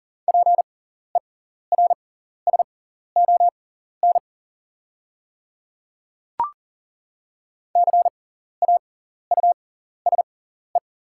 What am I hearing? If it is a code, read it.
Morse code, heard as a keyed tone, about 40 wpm, spaced as for 14 wpm, 690 Hz. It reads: PERSON CAUSE
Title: 6x Character Spacing - Top 500 Words - 40wpm